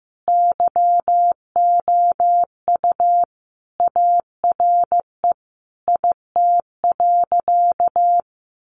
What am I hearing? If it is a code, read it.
Morse code, heard as a keyed tone, about 15 wpm, 700 Hz. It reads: YOU ARE IT.
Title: Additional Easy Phrases 15wpm